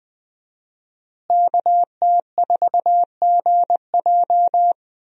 Morse code, heard as a keyed tone, about 20 wpm, 700 Hz. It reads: KT4GJ
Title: USA Call Signs - 5 Characters - Repeated 2x - 20wpm